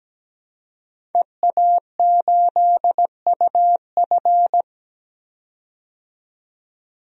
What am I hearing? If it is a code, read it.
Morse code, heard as a keyed tone, about 17 wpm, 700 Hz. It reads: EA8UF
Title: DXCC-100 Call Signs - 5 Characters - 17wpm